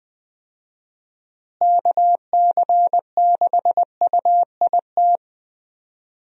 Morse code, heard as a keyed tone, about 19 wpm, 700 Hz. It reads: KC6UIT
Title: USA Call Signs - 6 Characters - Repeated 2x - 20wpm